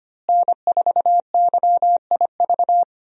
Morse code, heard as a keyed tone, about 25 wpm, 700 Hz. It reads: N4YIV
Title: USA Call Signs - 5 Characters - Repeated 2x - 25wpm